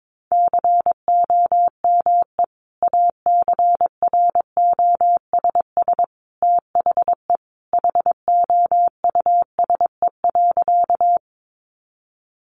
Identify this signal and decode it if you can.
Morse code, heard as a keyed tone, about 22 wpm, 700 Hz. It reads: COME ACROSS THE HOUSE.
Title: Sentences from Top 700 Words 22wpm